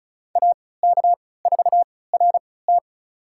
Morse code, heard as a keyed tone, about 36 wpm, spaced as for 20 wpm, 700 Hz. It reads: AK4RT
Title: USA Call Signs - 3x Character Spacing - 35wpm